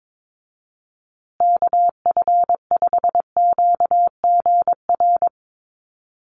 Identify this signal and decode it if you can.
Morse code, heard as a keyed tone, about 22 wpm, 700 Hz. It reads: KF5QGR
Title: USA Call Signs Repeated 3x 22wpm